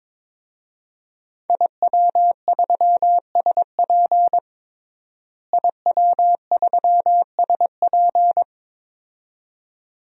Morse code, heard as a keyed tone, about 22 wpm, 700 Hz. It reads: IW3SP IW3SP
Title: DXCC-100 Call Signs - 5 Characters - Repeated 2x - 22wpm